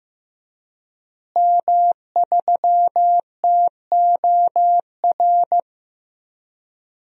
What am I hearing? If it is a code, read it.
Morse code, heard as a keyed tone, about 15 wpm, 700 Hz. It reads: M3TOR
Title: DXCC-100 Call Signs - 5 Characters - Repeated 3x - 15wpm